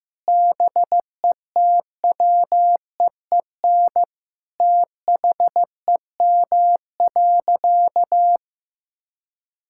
Morse code, heard as a keyed tone, about 16 wpm, 700 Hz. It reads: BETWEEN THEM.
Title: Sentences from Top 1000 Words - Review Part 6 - 15wpm